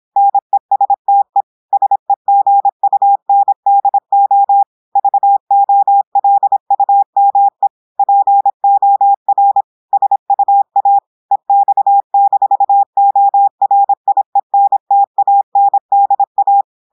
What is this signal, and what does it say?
Morse code, heard as a keyed tone, about 26 wpm, 800 Hz. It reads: NESTE SEGUNDO VOLUME POR SUA EX-ORIENTANDA
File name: audios / portugais / portugais texte en clair 1 vitesse 26 mots minutes